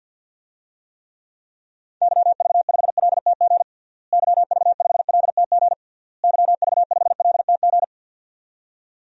Morse code, heard as a keyed tone, about 52 wpm, 700 Hz. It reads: YV5LTG YV5LTG YV5LTG
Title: DXCC-100 Call Signs - 6 Characters - Repeated 3x - 50wpm